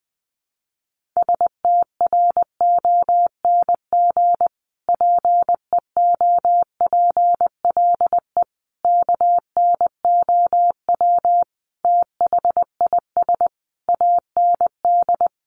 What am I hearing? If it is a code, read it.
Morse code, heard as a keyed tone, about 20 wpm, 700 Hz. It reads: STRONG PEOPLE KNOW THIS AND
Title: Sentences from Top 1500 Words 20wpm